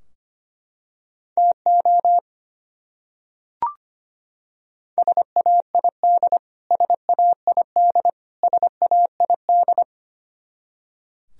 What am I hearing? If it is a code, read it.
Morse code, heard as a keyed tone, about 25 wpm, 700 Hz. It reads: TO SAID SAID SAID
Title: Top 100 Words Repeated 25wpm